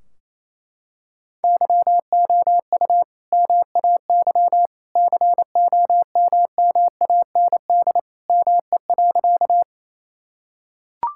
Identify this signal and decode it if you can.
Morse code, heard as a keyed tone, about 28 wpm, 700 Hz. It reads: YOU MAY COMMAND ME.
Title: Sentences from Top 1600 Words 28wpm